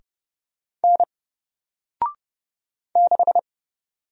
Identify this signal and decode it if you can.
Morse code, heard as a keyed tone, about 30 wpm, 700 Hz. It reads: N 6
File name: Single-Letter-Number-30wpm